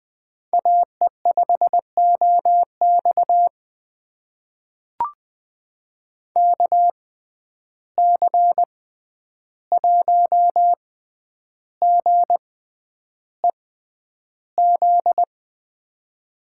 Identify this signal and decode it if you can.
Morse code, heard as a keyed tone, about 20 wpm, spaced as for 10 wpm, 700 Hz. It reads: AE5OX KC1GEZ
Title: USA Call Signs - 6x Character Spacing - 20wpm